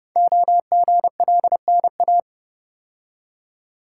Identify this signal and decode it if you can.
Morse code, heard as a keyed tone, about 30 wpm, 700 Hz. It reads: OGLNA
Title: Code Groups - 5 Letters - 30wpm